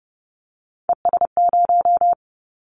Morse code, heard as a keyed tone, about 29 wpm, 700 Hz. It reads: ES0
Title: Callsign Prefixes for Top 100 DXCC - 30wpm